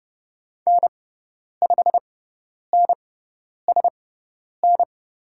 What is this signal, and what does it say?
Morse code, heard as a keyed tone, about 30 wpm, spaced as for 10 wpm, 700 Hz. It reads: N5NSN
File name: Lesson 18g - Letter m - Call Signs 30wpm